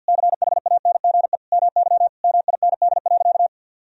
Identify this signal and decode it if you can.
Morse code, heard as a keyed tone, about 48 wpm, 700 Hz. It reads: CHANGE MY MIND.
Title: Sentences from Top 1000 Words - Review Part 1 - 50wpm